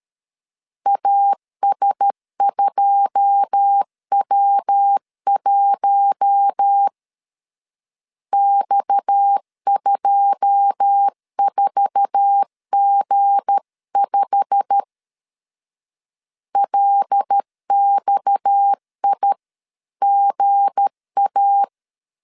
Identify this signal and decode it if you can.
Morse code, heard as a keyed tone, about 13 wpm, 800 Hz. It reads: AS2W1 X24G5 LXIGA
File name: CW / DET / 1 9 1500C S A 12wpm 3800